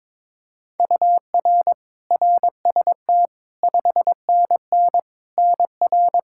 Morse code, heard as a keyed tone, about 22 wpm, 700 Hz. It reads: UR RST 5NN NR
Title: Practice QSO - Exchange 1 - Beginning - 22wpm